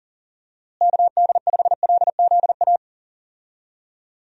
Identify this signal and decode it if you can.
Morse code, heard as a keyed tone, about 40 wpm, 700 Hz. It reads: KD5LZA
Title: USA Call Signs - 6 Characters - Repeated 3x - 40wpm